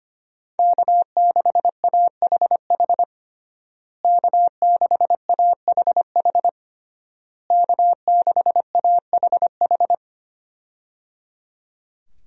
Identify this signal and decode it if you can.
Morse code, heard as a keyed tone, about 25 wpm, 700 Hz. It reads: K6AHH K6AHH K6AHH
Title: USA Call Signs - 5 Characters - Repeated 3x - 25wpm